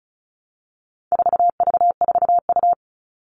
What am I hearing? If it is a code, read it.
Morse code, heard as a keyed tone, about 34 wpm, 700 Hz. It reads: 4V4U
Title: All Cut Numbers - 4-Digit 35wpm